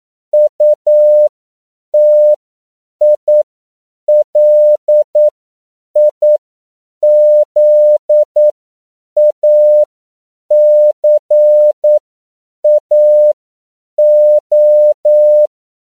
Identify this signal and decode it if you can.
Morse code, heard as a keyed tone, about 9 wpm, spaced as for 7 wpm, 600 Hz. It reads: UTILIZACAO
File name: CW / 10PPM